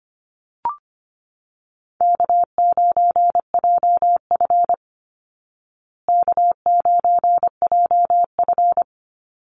Morse code, heard as a keyed tone, about 25 wpm, 700 Hz. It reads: K9JF K9JF
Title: Easy Call Signs Repeated 3x - 25wpm